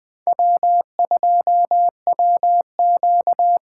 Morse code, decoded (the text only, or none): W2WQ